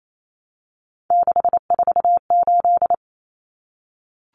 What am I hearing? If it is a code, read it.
Morse code, heard as a keyed tone, about 28 wpm, 700 Hz. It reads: 648
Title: Numbers - 3-Digit 28wpm